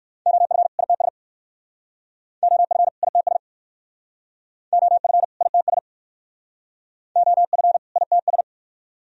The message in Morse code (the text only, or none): OF ITS OF ITS OF ITS OF ITS